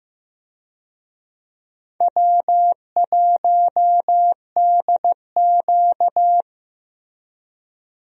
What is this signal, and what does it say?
Morse code, heard as a keyed tone, about 15 wpm, 700 Hz. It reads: W1DQ
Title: USA Call Signs - 4 Characters - Repeated 2x - 15wpm